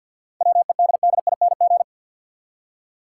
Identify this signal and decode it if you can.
Morse code, heard as a keyed tone, about 52 wpm, 700 Hz. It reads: WEDDING
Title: Seven Letter Words - 50wpm